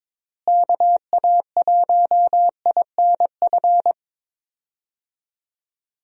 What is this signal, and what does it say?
Morse code, heard as a keyed tone, about 22 wpm, 700 Hz. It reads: KA1INF
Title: USA Call Signs - 6 Characters - Repeated 3x - 22wpm